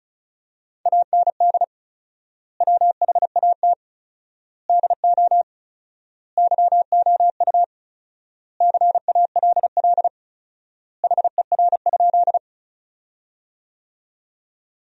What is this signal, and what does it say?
Morse code, heard as a keyed tone, about 36 wpm, 700 Hz. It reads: AND WHAT DO YOU CALL HER?
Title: 4x Word-Spacing - Sentences from Top 100 Words - 35wpm